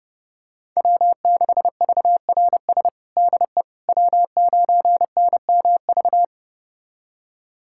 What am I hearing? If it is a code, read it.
Morse code, heard as a keyed tone, about 30 wpm, 700 Hz. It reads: W6VRS DE W9NMV
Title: USA Callsign DE Callsign - 5 Characters - 30wpm